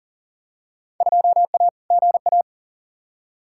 Morse code, heard as a keyed tone, about 40 wpm, 700 Hz. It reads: 2A GA
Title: Field Day Exchanges Encore - 40wpm